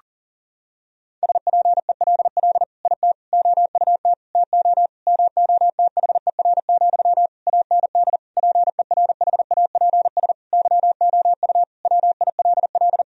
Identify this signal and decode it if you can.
Morse code, heard as a keyed tone, about 37 wpm, 700 Hz. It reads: SPELL IT OUT TO MOTHER, AND PERHAPS YOU WILL
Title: Sentences from Top 1000 Words - Review Part 1 - 40wpm